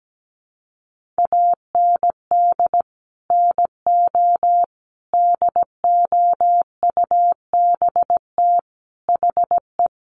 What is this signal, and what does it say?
Morse code, heard as a keyed tone, about 17 wpm, 700 Hz. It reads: AND NO DOUBT HE
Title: Sentences from Top 1500 Words 17wpm